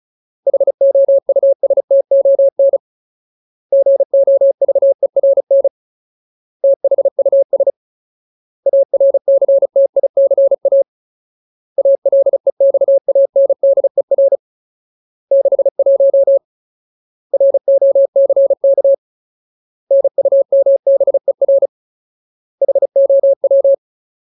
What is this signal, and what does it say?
Morse code, heard as a keyed tone, about 34 wpm, 550 Hz. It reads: HOUSTON GOVERN THUS ARCTICA ALEXANDER 61 ROCK NUMBER HOW
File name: Ultimate Word List / 35WPM Ultimate